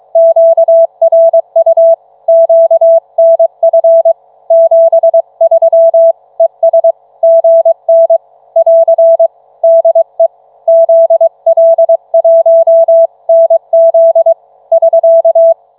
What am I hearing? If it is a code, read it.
Morse code, heard as a keyed tone, about 23 wpm, 660 Hz. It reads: QRU QNF 73 ES GN <AR> DE ZL1NZ <SK>